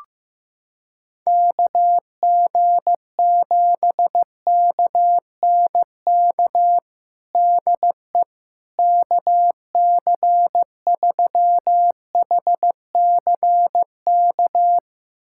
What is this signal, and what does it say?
Morse code, heard as a keyed tone, about 15 wpm, 700 Hz. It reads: KG7KNK DE KC3HCK
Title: USA Callsign DE Callsign - 6 Characters - 15wpm